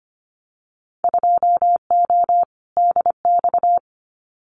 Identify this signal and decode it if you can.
Morse code, heard as a keyed tone, about 25 wpm, 700 Hz. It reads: 2O DX
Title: Winter Field Day Exchanges Encore 25wpm